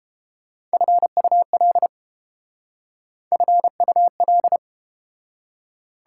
Morse code, heard as a keyed tone, about 32 wpm, 700 Hz. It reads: FUL FUL